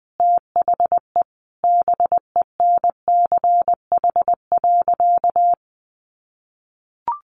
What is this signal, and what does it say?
Morse code, heard as a keyed tone, about 19 wpm, 700 Hz. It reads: THE BENCH.
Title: Sentences from Top 1400 Words 20wpm